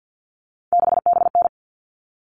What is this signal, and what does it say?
Morse code, heard as a keyed tone, about 48 wpm, 700 Hz. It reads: 6BN
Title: All Cut Numbers - 3-Digit 50wpm